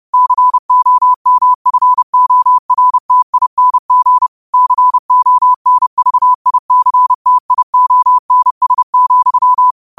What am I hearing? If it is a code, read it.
Morse code, heard as a keyed tone, about 30 wpm, 1000 Hz. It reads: COMFORTING CONVICTIONS,
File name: qotdfiles / 30wpm